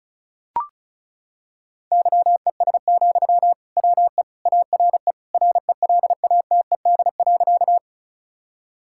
Morse code, heard as a keyed tone, about 36 wpm, 700 Hz. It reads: YES, WE ARE RELATED.
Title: Sentences from Top 1000 Words - Review Part 5 - 35wpm